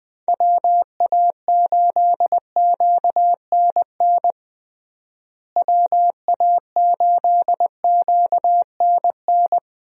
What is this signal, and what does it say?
Morse code, heard as a keyed tone, about 20 wpm, 700 Hz. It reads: WA8QNN WA8QNN
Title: USA Call Signs - 6 Characters - Repeated 3x - 20wpm